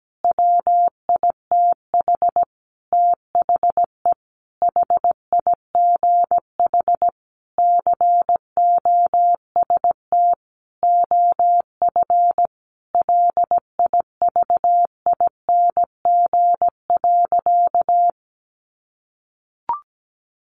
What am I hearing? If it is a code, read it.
Morse code, heard as a keyed tone, about 17 wpm, 700 Hz. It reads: WITH THE HIGH COST OF LIVING.
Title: Sentences from Top 1400 Words 17wpm